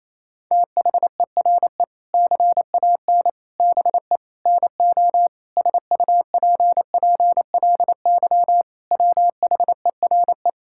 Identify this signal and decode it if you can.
Morse code, heard as a keyed tone, about 28 wpm, 700 Hz. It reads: THERE CAN BE NO SUPPLY WHERE
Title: Sentences from Top 1000 Words - Review Part 5 - 28wpm